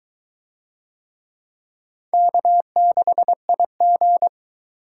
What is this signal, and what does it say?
Morse code, heard as a keyed tone, about 21 wpm, 700 Hz. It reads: K6IG